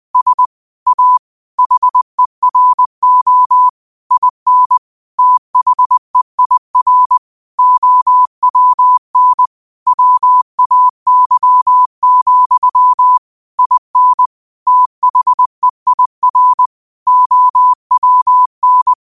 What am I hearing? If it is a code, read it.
Morse code, heard as a keyed tone, about 20 wpm, 1000 Hz. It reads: S A HERO IN THEIR OWN WAY, IN THEIR OWN
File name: 20wpm